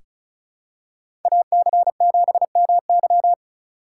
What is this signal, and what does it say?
Morse code, heard as a keyed tone, about 36 wpm, 700 Hz. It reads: AC7MY
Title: USA Call Signs - 5 Characters - 2x Character Spacing - 35wpm